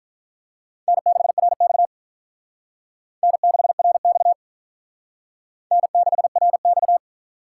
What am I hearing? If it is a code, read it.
Morse code, heard as a keyed tone, about 42 wpm, 700 Hz. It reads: N6RX N6RX N6RX
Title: Speed Racing - USA Call Signs 35wpm